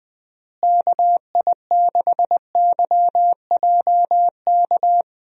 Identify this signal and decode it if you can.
Morse code, heard as a keyed tone, about 20 wpm, 700 Hz. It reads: KI6YJK
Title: USA Call Signs - 6 Characters - Repeated 3x - 20wpm